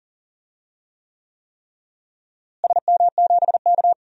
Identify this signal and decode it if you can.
Morse code, heard as a keyed tone, about 39 wpm, 700 Hz. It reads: SM7K
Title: DXCC-100 Call Signs - 4 Characters - Repeated 3x - 40wpm